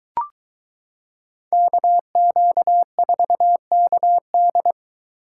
Morse code, heard as a keyed tone, about 23 wpm, 700 Hz. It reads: KQ4KD